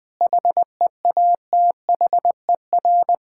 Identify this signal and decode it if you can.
Morse code, heard as a keyed tone, about 21 wpm, 700 Hz. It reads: HEATHER
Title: K1USN SST QSO Practice 20wpm